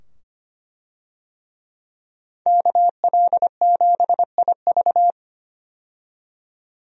Morse code, heard as a keyed tone, about 25 wpm, 700 Hz. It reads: KL7IV
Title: DXCC-100 Call Signs - 5 Characters - 25wpm